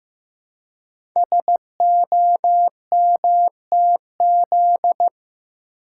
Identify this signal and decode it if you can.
Morse code, heard as a keyed tone, about 15 wpm, 700 Hz. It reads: SOMTZ
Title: Code Groups - 5 Letter:Numbers - 15wpm